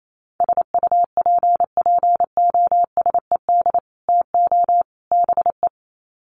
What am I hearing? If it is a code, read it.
Morse code, heard as a keyed tone, about 28 wpm, 700 Hz. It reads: SUPPOSED TO BE
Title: Sets of 3 Words 28wpm